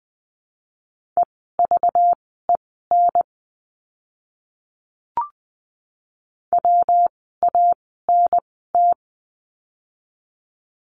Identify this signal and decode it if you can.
Morse code, heard as a keyed tone, about 20 wpm, spaced as for 14 wpm, 700 Hz. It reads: EVEN WANT
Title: 206 Lesson 33f - Letter j - Words 20wpm